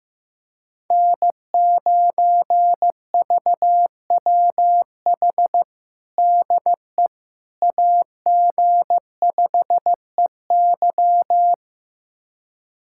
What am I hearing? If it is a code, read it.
Morse code, heard as a keyed tone, about 15 wpm, 700 Hz. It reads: N9VWH DE AG5EY
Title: USA Callsign DE Callsign - 5 Characters - 15wpm